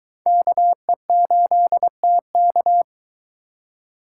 Morse code, heard as a keyed tone, about 24 wpm, 700 Hz. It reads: KE8TK